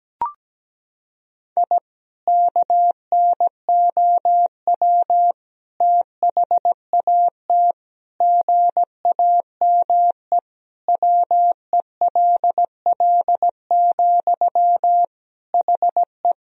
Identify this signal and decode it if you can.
Morse code, heard as a keyed tone, about 17 wpm, 700 Hz. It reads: I KNOW THAT GAME WELL, HE
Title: Sentences from Top 1000 Words - Review Part 3 - 17wpm